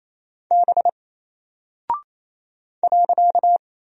Morse code, heard as a keyed tone, about 27 wpm, 700 Hz. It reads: B .
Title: Single Character (Letter:Number:Punctuation:Prosign) 28wpm